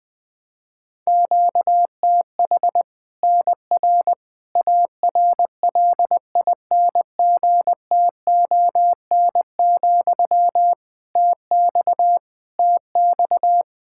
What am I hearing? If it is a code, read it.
Morse code, heard as a keyed tone, about 21 wpm, 700 Hz. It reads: QTH NR ARLINGTON, TX TX
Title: Practice QSO - Exchange 1 - Beginning - 20wpm